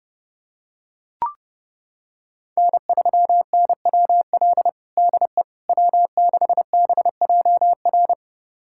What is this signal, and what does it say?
Morse code, heard as a keyed tone, about 30 wpm, 700 Hz. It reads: N3NWL DE W6BJR
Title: USA Callsign DE Callsign - 5 Characters - 30wpm